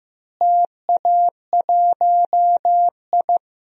Morse code, heard as a keyed tone, about 16 wpm, 700 Hz. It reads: TA1I
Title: DXCC-100 Call Signs - 4 Characters - Repeated 3x - 15wpm